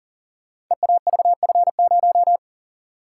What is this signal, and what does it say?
Morse code, heard as a keyed tone, about 38 wpm, 700 Hz. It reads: EAVF0
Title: Code Groups - 5 Letter:Numbers - 40wpm